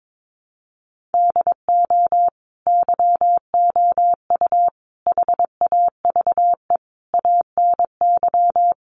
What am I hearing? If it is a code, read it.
Morse code, heard as a keyed tone, about 22 wpm, 700 Hz. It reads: DO YOU HAVE ANY
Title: Sentences from Top 1100 Words 22wpm